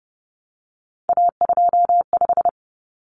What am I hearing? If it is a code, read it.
Morse code, heard as a keyed tone, about 30 wpm, 700 Hz. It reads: A25
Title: Common Cut Numbers - 3-Digit 30wpm